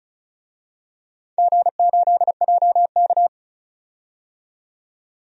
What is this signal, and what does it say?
Morse code, heard as a keyed tone, about 36 wpm, 700 Hz. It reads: G8JK
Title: DXCC-100 Call Signs - 4 Characters - 35wpm